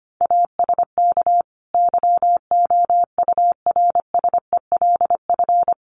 Morse code, heard as a keyed tone, about 25 wpm, 700 Hz. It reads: ASK YOURSELF
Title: Sentences from Top 600 Words 25wpm